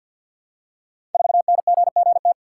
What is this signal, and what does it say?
Morse code, heard as a keyed tone, about 52 wpm, 700 Hz. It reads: 4NGGT
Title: Code Groups - 5 Letter:Numbers - 50wpm